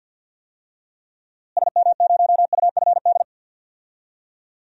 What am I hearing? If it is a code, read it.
Morse code, heard as a keyed tone, about 52 wpm, 700 Hz. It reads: SM0UFD
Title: DXCC-100 Call Signs - 6 Characters - Repeated 3x - 50wpm